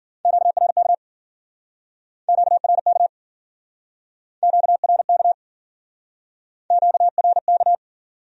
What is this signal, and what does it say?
Morse code, heard as a keyed tone, about 53 wpm, 700 Hz. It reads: QRK QRK QRK QRK